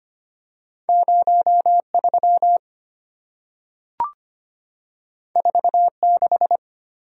Morse code, heard as a keyed tone, about 25 wpm, 700 Hz. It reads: 03 46
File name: Two Letter-Number Combinations - 25wpm